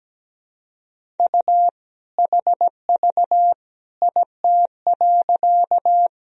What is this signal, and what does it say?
Morse code, heard as a keyed tone, about 17 wpm, 700 Hz. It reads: U HV IT.
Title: Sentences from Top 100 Words and Abbreviations 17wpm